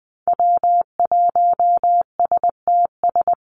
Morse code, heard as a keyed tone, about 20 wpm, 700 Hz. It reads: W1STS
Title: USA Call Signs 20wpm